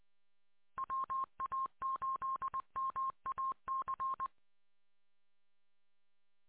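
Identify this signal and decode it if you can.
Morse code, heard as a keyed tone, about 20 wpm, 1070 Hz. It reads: WA8MAC
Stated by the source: Morse code recorded on 147.200 MHz.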